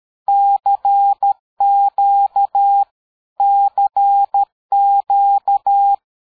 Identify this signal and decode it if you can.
Morse code, heard as a keyed tone, about 13 wpm, 800 Hz. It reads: CQ CQ
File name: dk5ke / QSO 60BpM